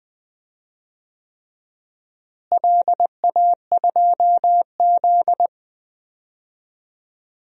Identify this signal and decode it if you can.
Morse code, heard as a keyed tone, about 21 wpm, 700 Hz. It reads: LA2Z
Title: DXCC-100 Call Signs - 4 Characters - Repeated 3x - 20wpm